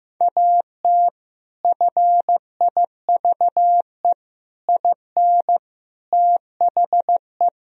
Morse code, heard as a keyed tone, about 15 wpm, 700 Hz. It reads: AT FIVE IN THE
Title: Sentences from Top 1000 Words - Review Part 2 - 15wpm